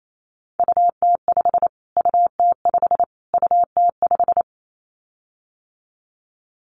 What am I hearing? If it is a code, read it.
Morse code, heard as a keyed tone, about 28 wpm, 700 Hz. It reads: UT5 UT5 UT5
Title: Callsign Prefixes for Top 25 DXCC - Repeated 3x - 28wpm